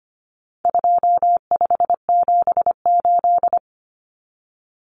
Morse code, heard as a keyed tone, about 25 wpm, 700 Hz. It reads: 2578
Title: Common Cut Numbers - 4-Digit 25wpm